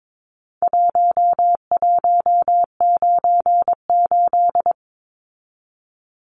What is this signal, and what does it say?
Morse code, heard as a keyed tone, about 22 wpm, 700 Hz. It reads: 1198
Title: Numbers - Contest Style - 4-Digit 22wpm